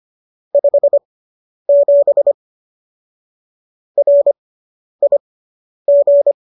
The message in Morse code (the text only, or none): 57 RIG